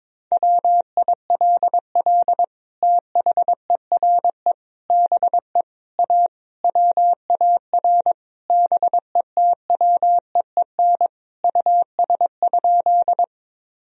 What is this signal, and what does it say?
Morse code, heard as a keyed tone, about 22 wpm, 700 Hz. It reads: WILL THERE BE A WAR BETWEEN US?
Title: Sentences from Top 1000 Words - Review Part 2 - 22wpm